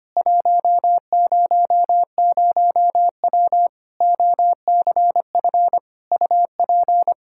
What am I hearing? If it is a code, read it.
Morse code, heard as a keyed tone, about 25 wpm, 700 Hz. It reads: 100W OCF UP
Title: Practice QSO - Exchange 2 - Info - 25wpm